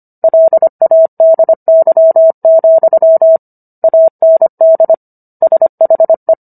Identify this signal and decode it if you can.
Morse code, heard as a keyed tone, about 25 wpm, 650 Hz. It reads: LADY, AND SHE